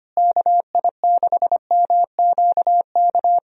Morse code, heard as a keyed tone, about 25 wpm, 700 Hz. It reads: KI6MQK
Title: USA Call Signs - 6 Characters - Repeated 3x - 25wpm